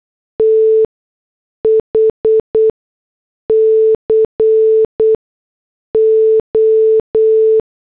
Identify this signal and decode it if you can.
Morse code, heard as a keyed tone, about 8 wpm, spaced as for 6 wpm, 440 Hz. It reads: TH CO